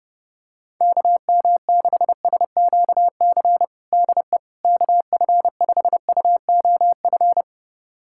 Morse code, heard as a keyed tone, about 30 wpm, 700 Hz. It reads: KM6SQC DE KF5UOF
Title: USA Callsign DE Callsign - 6 Characters - 30wpm